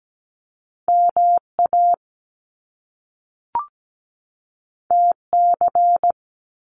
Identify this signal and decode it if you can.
Morse code, heard as a keyed tone, about 17 wpm, 700 Hz. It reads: MA TC
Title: Top Two Letter Combinations in Words 17wpm